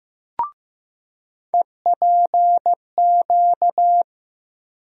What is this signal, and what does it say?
Morse code, heard as a keyed tone, about 15 wpm, 700 Hz. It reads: EPQ
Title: USA Call Sign Postfixes - 3 Letters 15wpm